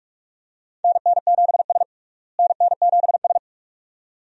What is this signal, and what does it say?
Morse code, heard as a keyed tone, about 46 wpm, 700 Hz. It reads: NN7S NN7S
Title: USA Call Signs - 4 Characters - Repeated 2x - 45wpm